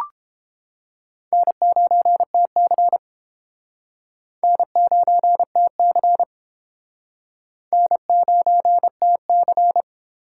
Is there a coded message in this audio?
Yes: Morse code from 1.3 to 9.8 s, heard as a keyed tone, about 30 wpm, 700 Hz. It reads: N9TC N9TC N9TC